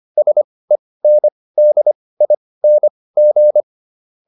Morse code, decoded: SENDING

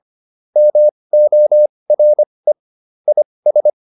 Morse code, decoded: MORE IS

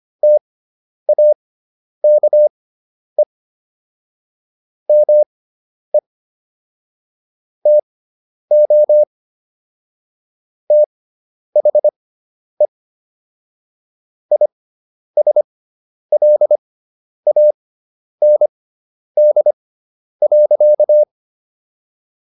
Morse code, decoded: TAKE ME TO THE ISLAND.

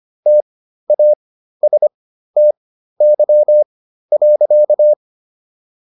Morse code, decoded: TASTY.

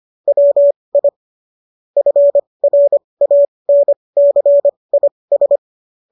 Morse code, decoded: WI FRANCIS

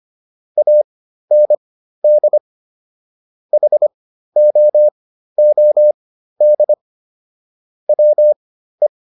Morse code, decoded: AND HOOD WE

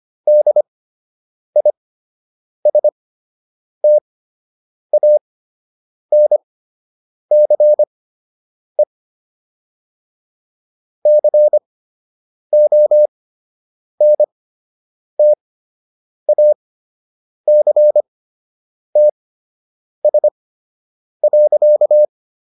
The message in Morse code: DISTANCE CONTACTS.

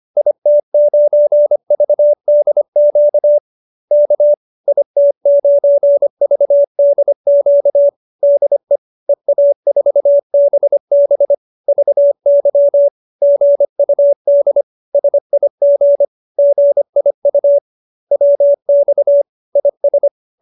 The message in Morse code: IT9VDQ K IT9VDQ DE EA4BB VY GUD SIG GIU WX IS

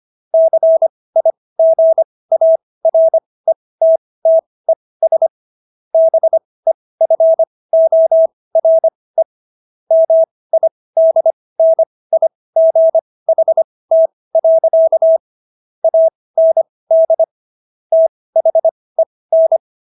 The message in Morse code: CIGARETTES BEFORE MIDNIGHT. AND THEN

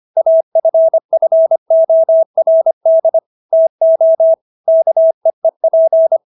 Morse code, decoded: AFFORD TO KEEP

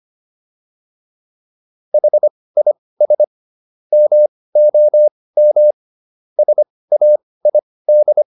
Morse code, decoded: HIS MOM SAID

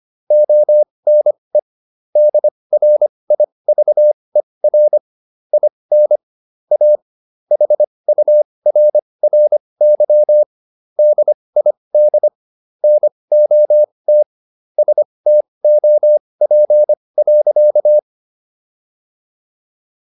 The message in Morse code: ONE DRIVER IN A HURRY DID NOT STOP.

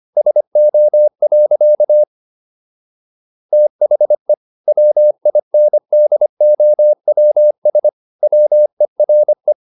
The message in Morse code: SO. THE WINDOWS WERE

